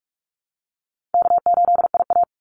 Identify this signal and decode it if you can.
Morse code, heard as a keyed tone, about 44 wpm, 700 Hz. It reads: K8IA
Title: Call Signs - 45wpm